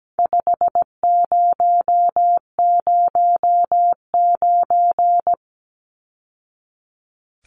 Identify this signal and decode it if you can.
Morse code, heard as a keyed tone, about 17 wpm, 700 Hz. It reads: 5009
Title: Numbers - 4-Digit 17wpm